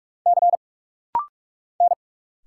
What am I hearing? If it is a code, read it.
Morse code, heard as a keyed tone, about 46 wpm, 690 Hz. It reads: C N
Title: Single Letters - Rapid-Fire 45wpm